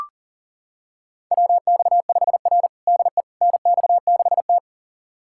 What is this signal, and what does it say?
Morse code, heard as a keyed tone, about 40 wpm, 700 Hz. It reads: WX5R DE NX6T
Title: USA Callsign DE Callsign - 4 Characters - 40wpm